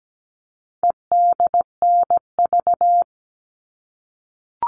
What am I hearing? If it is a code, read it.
Morse code, heard as a keyed tone, about 17 wpm, 700 Hz. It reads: EDNV
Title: All Cut Numbers - 4-Digit 17wpm